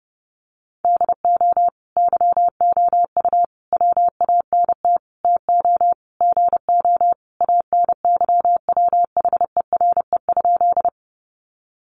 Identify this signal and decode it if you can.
Morse code, heard as a keyed tone, about 28 wpm, 700 Hz. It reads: DO YOU WANT TO GO ANYWHERE?
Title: Sentences from Top 1400 Words 30wpm